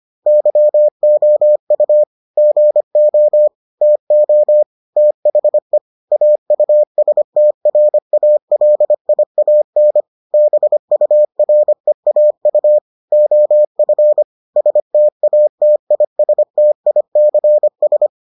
Morse code, heard as a keyed tone, about 25 wpm, 600 Hz. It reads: YOU GO TO THE AUSTRALIAN BUREAU OF STATISTICS